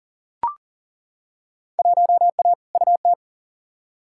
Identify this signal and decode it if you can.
Morse code, heard as a keyed tone, about 38 wpm, 700 Hz. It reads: 1A UT
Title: Field Day Exchanges - 40wpm